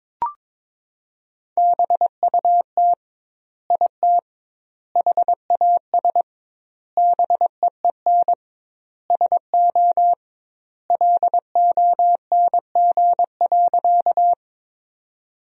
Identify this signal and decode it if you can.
Morse code, heard as a keyed tone, about 22 wpm, 700 Hz. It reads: BUT IT HAS BEEN SO LONG.
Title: 2x Word-Spacing - Sentences from Top 100 Words - 22wpm